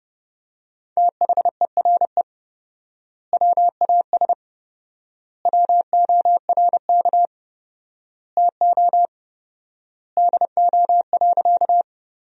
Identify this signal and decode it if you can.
Morse code, heard as a keyed tone, about 32 wpm, 700 Hz. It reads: THERE WAS WORK TO DO.
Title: 4x Word-Spacing - Sentences from Top 200 Words - 30wpm